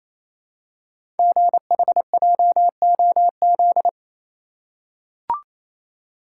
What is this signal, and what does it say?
Morse code, heard as a keyed tone, about 27 wpm, 700 Hz. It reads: GHJOZ
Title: Code Groups - 5 Letters - 28wpm